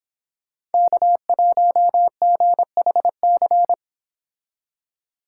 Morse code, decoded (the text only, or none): K1GHC